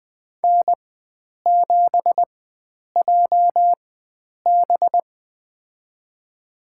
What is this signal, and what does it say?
Morse code, heard as a keyed tone, about 20 wpm, spaced as for 9 wpm, 700 Hz. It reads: N7JB
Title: USA Call Signs - 4 Characters - 4x Character Spacing - 20wpm